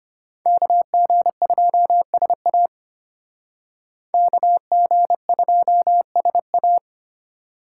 Morse code, heard as a keyed tone, about 32 wpm, 700 Hz. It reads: KG2SA KG2SA